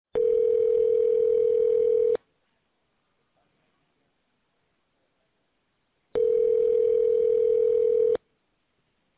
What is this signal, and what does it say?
Ringback tone (North America), 440 + 480 Hz.